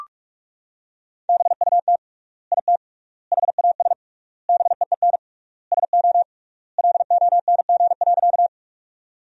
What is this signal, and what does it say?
Morse code, heard as a keyed tone, about 41 wpm, 700 Hz. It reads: BUT IT HAS BEEN SO LONG.
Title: 3x Word-Spacing - Sentences from Top 100 Words - 45wpm